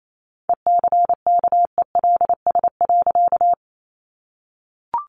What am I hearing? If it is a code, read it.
Morse code, heard as a keyed tone, about 28 wpm, 700 Hz. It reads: ECKELS.
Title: A Sound Of Thunder 28wpm